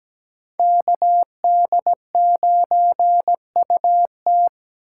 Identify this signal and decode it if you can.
Morse code, heard as a keyed tone, about 20 wpm, 700 Hz. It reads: KD9UT
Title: Speed Racing - USA 2x2 Call Signs 17wpm